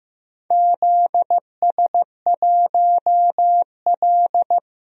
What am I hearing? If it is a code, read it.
Morse code, heard as a keyed tone, about 15 wpm, 700 Hz. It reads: ZS1L
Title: DXCC-100 Call Signs - 4 Characters - 15wpm